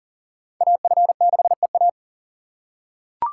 Morse code, heard as a keyed tone, about 40 wpm, 700 Hz. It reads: AF6EA
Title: USA Call Signs - 5 Characters - 4x Character Spacing - 40wpm